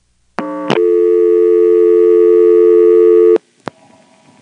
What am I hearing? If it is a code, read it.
Dial tone (North America), 350 + 440 Hz.